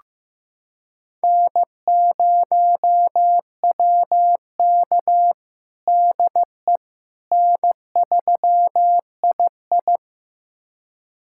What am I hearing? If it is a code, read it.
Morse code, heard as a keyed tone, about 15 wpm, 700 Hz. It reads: N0WK DE N3II
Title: USA Callsign DE Callsign - 4 Characters - 15wpm